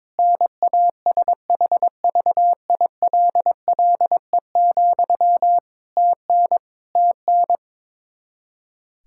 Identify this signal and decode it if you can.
Morse code, heard as a keyed tone, about 22 wpm, 700 Hz. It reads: NASHVILLE, TN TN
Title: Practice QSO - Exchange 1 - Beginning - 22wpm